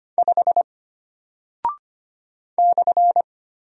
Morse code, heard as a keyed tone, about 25 wpm, 700 Hz. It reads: 5 /
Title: Single Character (Letter:Number:Punctuation:Prosign) Alt 25wpm